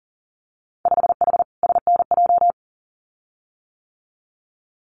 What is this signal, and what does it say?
Morse code, heard as a keyed tone, about 42 wpm, 700 Hz. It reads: 5H SNJ
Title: Winter Field Day Exchanges Encore 40wpm